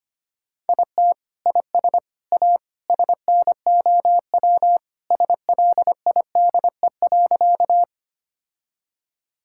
Morse code, heard as a keyed tone, about 25 wpm, 700 Hz. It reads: IT IS A SNOW SLIDE.
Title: Sentences from Top 1600 Words 25wpm